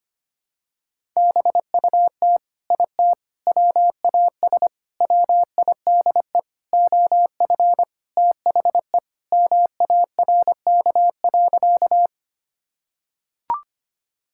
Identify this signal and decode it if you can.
Morse code, heard as a keyed tone, about 25 wpm, 700 Hz. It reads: BUT IT WAS WIDE OF THE MARK.
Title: Sentences from Top 1000 Words - Review Part 6 - 25wpm